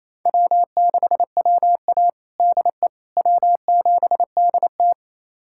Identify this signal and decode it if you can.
Morse code, heard as a keyed tone, about 28 wpm, 700 Hz. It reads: W6WA DE W7DT
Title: USA Callsign DE Callsign - 4 Characters - 28wpm